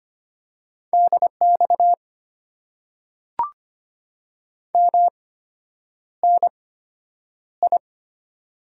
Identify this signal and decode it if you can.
Morse code, heard as a keyed tone, about 25 wpm, spaced as for 9 wpm, 700 Hz. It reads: DX MNI
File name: Top 90 QSO Elements - 8x Character Spacing - 25wpm